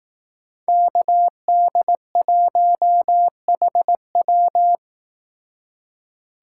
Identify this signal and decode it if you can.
Morse code, heard as a keyed tone, about 18 wpm, 700 Hz. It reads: KD1HW
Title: Speed Racing - USA Call Signs 15wpm